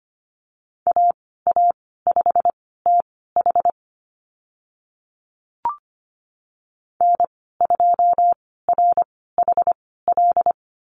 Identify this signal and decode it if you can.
Morse code, heard as a keyed tone, about 25 wpm, spaced as for 16 wpm, 690 Hz. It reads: AA5TH N2RHL
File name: Lesson 15g - Number 5 - Call Signs 25wpm